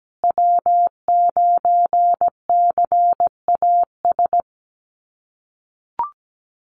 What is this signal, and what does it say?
Morse code, heard as a keyed tone, about 17 wpm, 700 Hz. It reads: W9CAS
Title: USA Call Signs Repeated 2x 17wpm